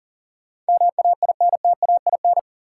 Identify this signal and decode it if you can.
Morse code, heard as a keyed tone, about 40 wpm, 700 Hz. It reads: MAINTAIN
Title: Sentences from Top 1000 Words - Review Part 5 - New Word Drill - 40wpm